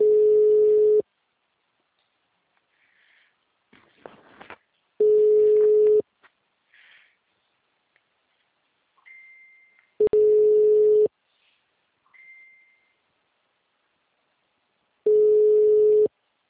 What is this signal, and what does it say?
Ringback tone (Europe, 425 Hz).
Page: Cordeiropolis - SP